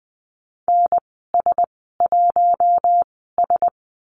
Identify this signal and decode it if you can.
Morse code, heard as a keyed tone, about 20 wpm, spaced as for 15 wpm, 700 Hz. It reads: NS1S
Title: 140 Lesson 23g - Letter f - Call Signs 20wpm